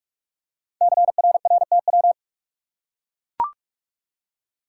Morse code, heard as a keyed tone, about 44 wpm, 700 Hz. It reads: CRRTW
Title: Code Groups - 5 Letters - 45wpm